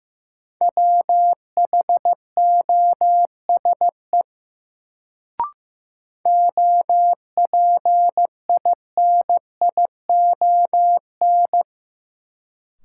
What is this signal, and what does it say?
Morse code, heard as a keyed tone, about 15 wpm, 700 Hz. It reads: WHOSE OPINION
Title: Sentences from Top 1200 Words - New Word Drill Alt - 15wpm